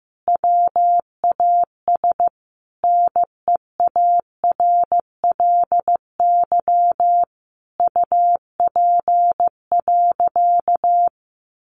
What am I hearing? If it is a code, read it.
Morse code, heard as a keyed tone, about 15 wpm, 700 Hz. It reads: WAS NEARLY UP.
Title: Sentences from Top 1300 Words 15wpm